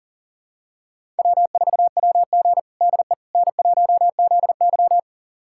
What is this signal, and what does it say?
Morse code, heard as a keyed tone, about 40 wpm, 700 Hz. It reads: W4WG DE N1ZY
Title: USA Callsign DE Callsign - 4 Characters - 40wpm